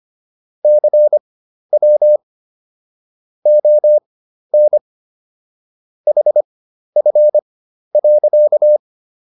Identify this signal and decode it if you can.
Morse code, heard as a keyed tone, about 25 wpm, spaced as for 12 wpm, 600 Hz. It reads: CW ON HF.